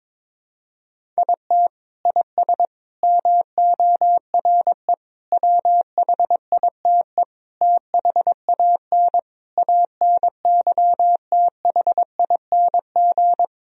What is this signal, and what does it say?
Morse code, heard as a keyed tone, about 21 wpm, 700 Hz. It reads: IT IS MORE WHITE THAN ANYTHING
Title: Sentences from Top 1000 Words - Review Part 1 - 22wpm